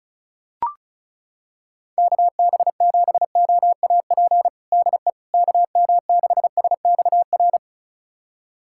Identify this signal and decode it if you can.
Morse code, heard as a keyed tone, about 34 wpm, 700 Hz. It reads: KB7OAP DE KM6SXR
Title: USA Callsign DE Callsign - 6 Characters - 35wpm